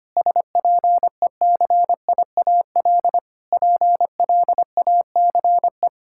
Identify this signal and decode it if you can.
Morse code, heard as a keyed tone, about 25 wpm, 700 Hz. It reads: SPECIAL PLACE
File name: Sets of 9 Words - 25wpm